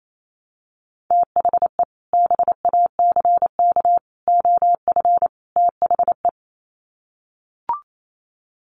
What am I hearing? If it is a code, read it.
Morse code, heard as a keyed tone, about 28 wpm, 700 Hz. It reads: THE BACK OF THE
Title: Sets of 4 Words 28wpm